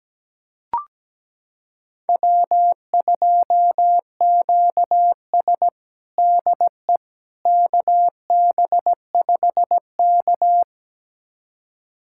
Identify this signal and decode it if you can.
Morse code, heard as a keyed tone, about 17 wpm, 700 Hz. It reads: W2QS DE KB5K
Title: USA Callsign DE Callsign - 4 Characters - 17wpm